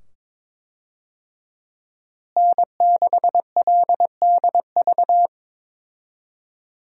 Morse code, heard as a keyed tone, about 21 wpm, 700 Hz. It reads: N6LDV
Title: USA Call Signs - 5 Characters - Repeated 2x - 22wpm